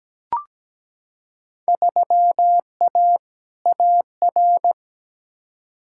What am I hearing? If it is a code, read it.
Morse code, heard as a keyed tone, about 17 wpm, 700 Hz. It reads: 3A AR
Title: Field Day Exchanges - 17wpm